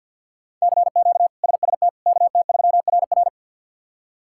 Morse code, heard as a keyed tone, about 51 wpm, 700 Hz. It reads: CQ SST KT3RR
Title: K1USN SST QSO Practice - 50wpm